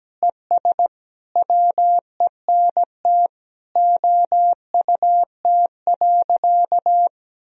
Morse code, heard as a keyed tone, about 17 wpm, 700 Hz. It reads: ES WENT OUT.
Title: Sentences from Top 300 Words and Abbreviations 17wpm